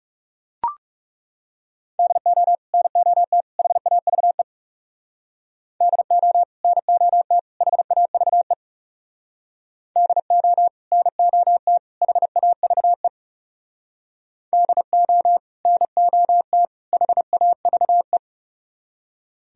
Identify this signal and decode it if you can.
Morse code, heard as a keyed tone, about 36 wpm, 700 Hz. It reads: DO NOT HAVE DO NOT HAVE DO NOT HAVE DO NOT HAVE
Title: Speed-Racing - Sets of 3 Words 30wpm